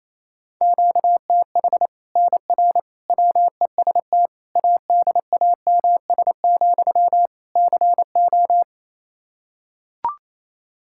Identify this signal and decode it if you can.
Morse code, heard as a keyed tone, about 28 wpm, 700 Hz. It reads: QTH NR WEST ADAMS, CO
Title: Practice QSO - Exchange 1 - Beginning - 28wpm